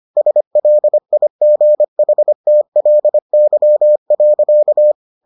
Morse code, heard as a keyed tone, about 25 wpm, 600 Hz. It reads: SLIGHTLY.